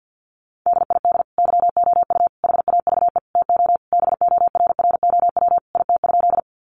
Morse code, heard as a keyed tone, about 50 wpm, 700 Hz. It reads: DID YOU HAVE TO BORROW IT?
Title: Sentences from Top 1500 Words 50wpm